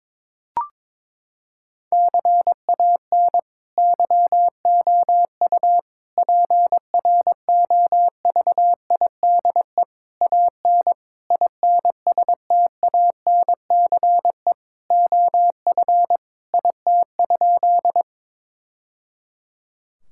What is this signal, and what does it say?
Morse code, heard as a keyed tone, about 22 wpm, 700 Hz. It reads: CAN YOU PROVIDE AN INSTANCE OF IT?
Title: Sentences from Top 1000 Words - Review Part 5 - 22wpm